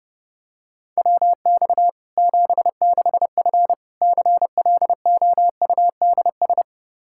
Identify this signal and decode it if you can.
Morse code, heard as a keyed tone, about 30 wpm, 700 Hz. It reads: WX 76F CLOUDS
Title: Practice QSO - Exchange 2 - Info - 30wpm